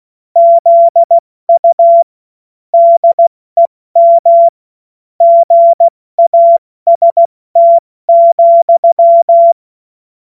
Morse code, heard as a keyed tone, about 16 wpm, spaced as for 14 wpm, 680 Hz. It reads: ZU DEM GAST,
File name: Morsekurs / Witze / Witz 11. Mai 2021, 70 F80